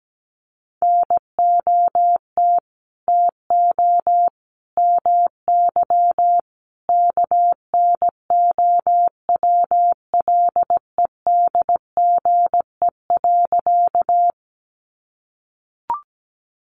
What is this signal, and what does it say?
Morse code, heard as a keyed tone, about 17 wpm, 700 Hz. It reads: NOT TO MY KNOWLEDGE.
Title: Sentences from Top 600 Words 17wpm